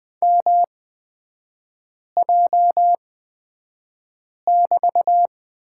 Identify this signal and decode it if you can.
Morse code, heard as a keyed tone, about 20 wpm, spaced as for 5 wpm, 700 Hz. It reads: MJ<BT>
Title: 243 Lesson 39e - Letter z - Three Characters 20wpm